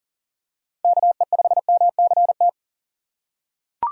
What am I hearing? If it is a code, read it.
Morse code, heard as a keyed tone, about 40 wpm, 700 Hz. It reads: KE5MCT
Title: USA Call Signs - 6x Character Spacing - 40wpm